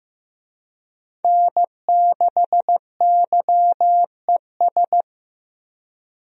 Morse code, heard as a keyed tone, about 15 wpm, 700 Hz. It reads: N6YES
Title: USA Call Signs - 5 Characters - 15wpm